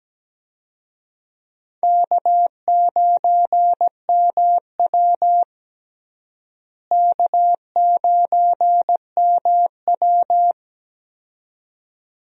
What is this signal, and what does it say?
Morse code, heard as a keyed tone, about 17 wpm, 700 Hz. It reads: K9MW K9MW
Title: USA Call Signs - 4 Characters - Repeated 2x - 17wpm